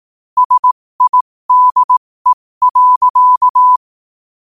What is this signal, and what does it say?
Morse code, heard as a keyed tone, about 18 wpm, spaced as for 15 wpm, 1000 Hz. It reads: SIDE.
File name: qotdfiles / 15wpm